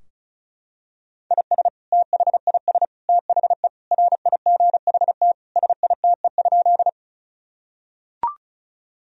Morse code, heard as a keyed tone, about 34 wpm, 700 Hz. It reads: IS THIS THE RIGHT SITE?
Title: Sentences from Top 1000 Words - Review Part 4 - 35wpm